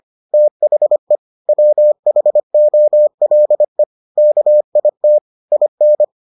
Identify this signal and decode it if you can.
Morse code, heard as a keyed tone, about 25 wpm, 600 Hz. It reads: THE WHOLE KIT IN